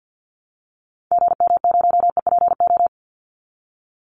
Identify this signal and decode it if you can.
Morse code, heard as a keyed tone, about 50 wpm, 700 Hz. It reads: GM0EPO
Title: Call Signs Repeated 2x - 50wpm